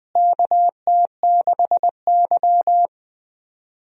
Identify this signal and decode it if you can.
Morse code, heard as a keyed tone, about 20 wpm, 700 Hz. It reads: KT6Y
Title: USA Call Signs - 4 Characters - Repeated 3x - 20wpm